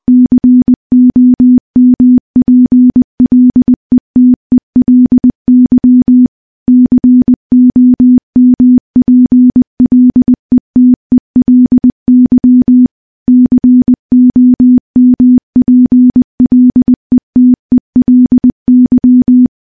Morse code, heard as a keyed tone, about 20 wpm, 250 Hz. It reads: COMPLETELY COMPLETELY COMPLETELY
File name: morse-training-20WPM